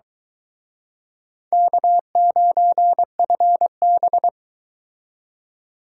Morse code, heard as a keyed tone, about 24 wpm, 700 Hz. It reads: K9FB